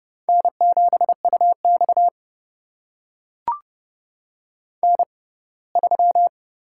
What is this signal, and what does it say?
Morse code, heard as a keyed tone, about 30 wpm, spaced as for 20 wpm, 700 Hz. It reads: N7UX N3
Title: USA Call Signs - 4 Characters - 6x Character Spacing - 30wpm